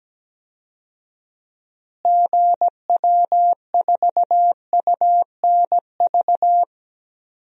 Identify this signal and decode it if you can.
Morse code, heard as a keyed tone, about 17 wpm, 700 Hz. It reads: GW4UNV
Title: DXCC-100 Call Signs - 6 Characters - Repeated 3x - 17wpm